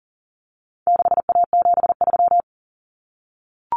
Morse code, heard as a keyed tone, about 40 wpm, 700 Hz. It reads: 6A73
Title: Common Cut Numbers - 4-Digit 40wpm